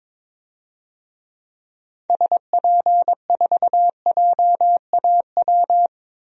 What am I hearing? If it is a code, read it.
Morse code, heard as a keyed tone, about 22 wpm, 700 Hz. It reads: SP4JAW
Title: DXCC-100 Call Signs - 6 Characters - 22wpm